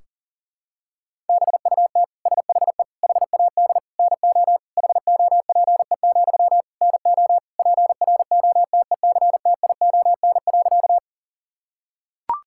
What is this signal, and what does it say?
Morse code, heard as a keyed tone, about 40 wpm, 700 Hz. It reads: BUT SHE HAD NO HOPE, NO PROTECTION.
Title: Sentences from Top 1600 Words 40wpm